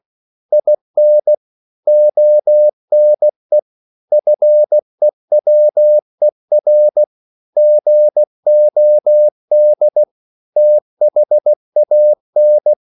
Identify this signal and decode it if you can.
Morse code, heard as a keyed tone, about 16 wpm, 600 Hz. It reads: IN ONE FEWER GOD THAN
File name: CW / Practice / EnglishQuotations0416